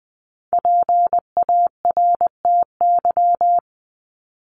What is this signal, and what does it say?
Morse code, heard as a keyed tone, about 20 wpm, 700 Hz. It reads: PARTY
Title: ES:S Suffix - Top 100 Words with Base Words 20wpm